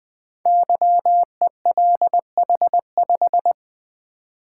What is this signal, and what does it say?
Morse code, heard as a keyed tone, about 20 wpm, 700 Hz. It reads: YELH5
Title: Code Groups - 5 Letter:Numbers - 20wpm